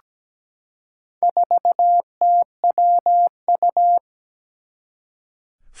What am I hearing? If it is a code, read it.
Morse code, heard as a keyed tone, about 17 wpm, 700 Hz. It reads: 4TWU
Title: Four Challenging Letter-Number Combinations 17wpm